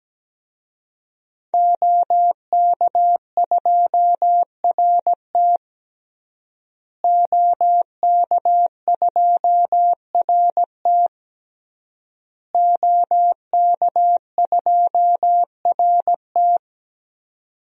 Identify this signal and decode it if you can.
Morse code, heard as a keyed tone, about 17 wpm, 700 Hz. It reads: OK2RT OK2RT OK2RT
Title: DXCC-100 Call Signs - 5 Characters - Repeated 3x - 17wpm